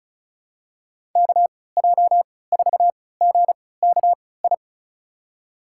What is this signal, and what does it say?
Morse code, heard as a keyed tone, about 36 wpm, spaced as for 20 wpm, 700 Hz. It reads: KJ4GKI
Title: USA Call Signs - 6 Characters - 3x Character Spacing - 35wpm